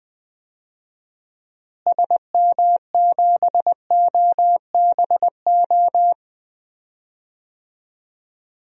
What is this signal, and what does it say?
Morse code, heard as a keyed tone, about 21 wpm, 700 Hz. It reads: SM7OBO
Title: DXCC-100 Call Signs - 6 Characters - Repeated 3x - 20wpm